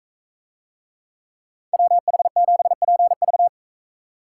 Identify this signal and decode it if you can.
Morse code, heard as a keyed tone, about 41 wpm, 700 Hz. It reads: WH7PV